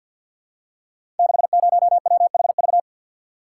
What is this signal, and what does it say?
Morse code, heard as a keyed tone, about 52 wpm, 700 Hz. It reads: 60WHV
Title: Code Groups - 5 Letter:Numbers - 50wpm